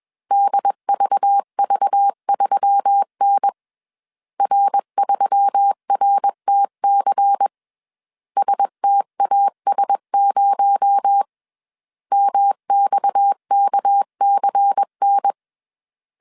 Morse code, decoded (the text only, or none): DVV3N R3RTC STAS0 MXKCN